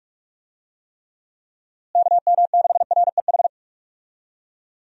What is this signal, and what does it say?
Morse code, heard as a keyed tone, about 44 wpm, 700 Hz. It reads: KM6REH
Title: USA Call Signs - 6 Characters - Repeated 2x - 45wpm